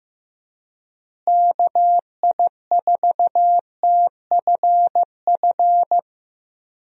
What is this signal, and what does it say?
Morse code, heard as a keyed tone, about 15 wpm, 700 Hz. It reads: KI4TFF
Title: USA Call Signs - 6 Characters - Repeated 3x - 15wpm